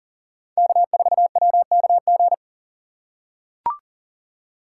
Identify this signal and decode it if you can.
Morse code, heard as a keyed tone, about 39 wpm, 700 Hz. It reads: K4WKG
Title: USA Call Signs - 5 Characters - 2x Character Spacing - 40wpm